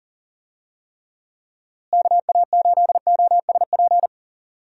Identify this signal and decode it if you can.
Morse code, heard as a keyed tone, about 40 wpm, 700 Hz. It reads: KA8OSP
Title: USA Call Signs - 6 Characters - Repeated 2x - 40wpm